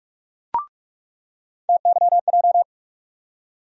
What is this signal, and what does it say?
Morse code, heard as a keyed tone, about 42 wpm, 700 Hz. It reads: TYJ
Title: USA Call Sign Postfixes - 3 Letters 45wpm